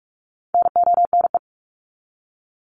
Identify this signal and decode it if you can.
Morse code, heard as a keyed tone, about 44 wpm, 700 Hz. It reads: NONE
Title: Top 1501-1750 Words 45wpm